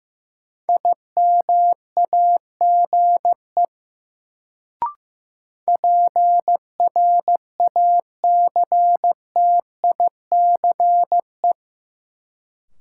Morse code, heard as a keyed tone, about 15 wpm, 700 Hz. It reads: IMAGE PRACTICE
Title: Sentences from Top 700 Words - New Word Drill - 15wpm